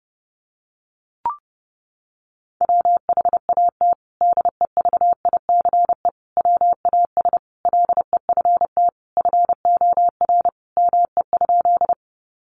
Morse code, heard as a keyed tone, about 30 wpm, 700 Hz. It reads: WHAT DEVICE WAS LEFT FOR ME?
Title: Sentences from Top 1500 Words 30wpm